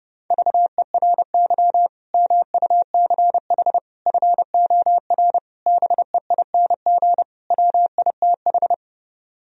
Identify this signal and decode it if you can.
Morse code, heard as a keyed tone, about 30 wpm, 700 Hz. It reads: VERY MUCH FOR BEING WITH